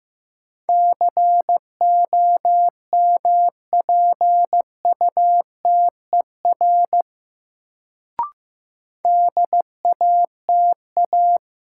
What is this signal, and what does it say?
Morse code, heard as a keyed tone, about 15 wpm, 700 Hz. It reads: COMPUTER DATA
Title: Sentences from Top 1000 Words - Review Part 3 - New Word Drill - 15wpm